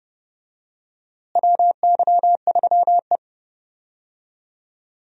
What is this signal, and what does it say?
Morse code, heard as a keyed tone, about 30 wpm, 700 Hz. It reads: WY3E
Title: USA Call Signs - 4 Characters - Repeated 3x - 30wpm